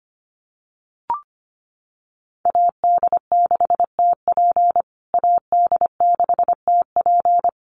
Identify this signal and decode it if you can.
Morse code, heard as a keyed tone, about 25 wpm, 700 Hz. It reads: AD6TP AD6TP
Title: USA Call Signs Repeated 3x 25wpm